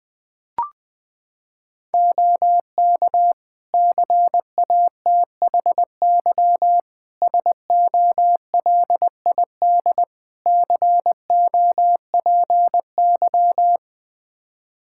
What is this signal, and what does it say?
Morse code, heard as a keyed tone, about 20 wpm, 700 Hz. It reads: OK CATHY SOLID COPY
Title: K1USN SST QSO Practice 20wpm